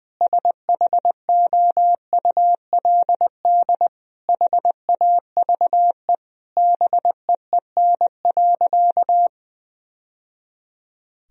Morse code, heard as a keyed tone, about 20 wpm, 700 Hz. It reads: SHOULD HAVE BEEN.
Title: Sentences from Top 1000 Words - Review Part 4 - 20wpm